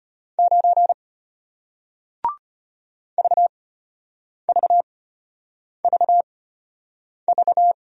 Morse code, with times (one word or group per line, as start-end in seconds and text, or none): 0.4-0.9 9
3.2-7.7 VVVV